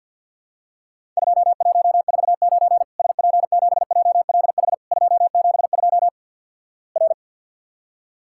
Morse code, heard as a keyed tone, about 50 wpm, 700 Hz. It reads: 2149 SP7JLH 272 R
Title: Worked All Europe DX Contest - QTC Exchange - 50wpm